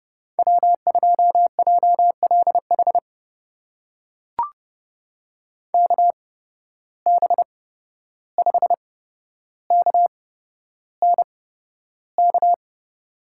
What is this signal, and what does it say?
Morse code, heard as a keyed tone, about 30 wpm, spaced as for 12 wpm, 700 Hz. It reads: W2JLH KB5KNK
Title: USA Call Signs - 8x Character Spacing - 30wpm